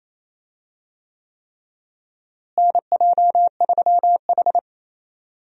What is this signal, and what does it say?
Morse code, heard as a keyed tone, about 27 wpm, 700 Hz. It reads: NJ3H
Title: USA Call Signs - 4 Characters - Repeated 3x - 28wpm